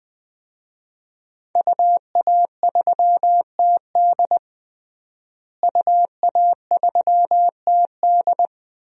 Morse code, heard as a keyed tone, about 20 wpm, 700 Hz. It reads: UA3TD UA3TD
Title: DXCC-100 Call Signs - 5 Characters - Repeated 3x - 20wpm